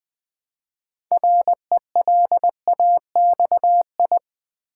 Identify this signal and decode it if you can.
Morse code, heard as a keyed tone, about 20 wpm, 700 Hz. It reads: RELAXI
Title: Build A Word - 3 Letters At A Time - 8-Letter Words - 20wpm